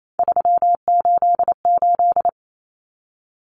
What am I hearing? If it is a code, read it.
Morse code, heard as a keyed tone, about 28 wpm, 700 Hz. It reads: 388
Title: Common Cut Numbers - 3-Digit 28wpm